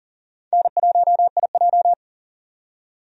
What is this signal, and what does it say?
Morse code, heard as a keyed tone, about 40 wpm, 700 Hz. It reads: N1IJ
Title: USA Call Signs - 4 Characters - 40wpm